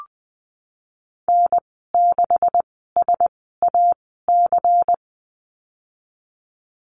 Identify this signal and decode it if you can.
Morse code, heard as a keyed tone, about 20 wpm, spaced as for 15 wpm, 700 Hz. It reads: N6SAC
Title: 161 Lesson 26g - Letter g - Call Signs 20wpm